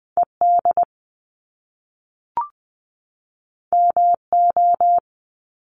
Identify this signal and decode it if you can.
Morse code, heard as a keyed tone, about 19 wpm, 700 Hz. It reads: ED MO
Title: Top Two Letter Combinations in Words 20wpm